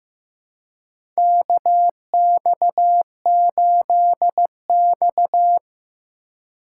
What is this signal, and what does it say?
Morse code, heard as a keyed tone, about 15 wpm, 700 Hz. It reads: KX8X
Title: K1USN SST QSO Practice - 15wpm